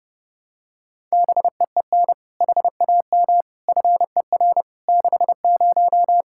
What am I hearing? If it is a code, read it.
Morse code, heard as a keyed tone, about 30 wpm, 700 Hz. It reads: BEEN HAM FER 60
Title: Practice QSO - Exchange 3 - More Info - 30wpm